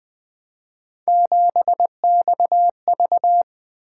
Morse code, heard as a keyed tone, about 20 wpm, 700 Hz. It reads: 7XV
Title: Three Letter-Number Combinations - 20wpm